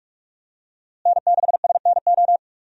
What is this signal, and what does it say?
Morse code, heard as a keyed tone, about 46 wpm, 700 Hz. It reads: N6SNO
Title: USA Call Signs - 5 Characters - Repeated 2x - 45wpm